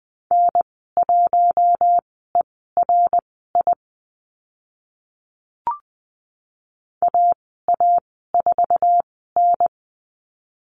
Morse code, heard as a keyed tone, about 20 wpm, spaced as for 15 wpm, 700 Hz. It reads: N1ERI AA4N
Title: 076 Lesson 13g - Letter L - Call Signs 20wpm